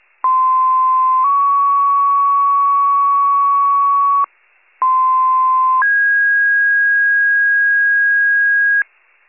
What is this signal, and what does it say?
Two-tone page, heard as a one-second tone, then a three-second tone, about 1040 then 1120 Hz, about 1040 then 1650 Hz, with noise. It is heard from 0.2 to 8.8 s.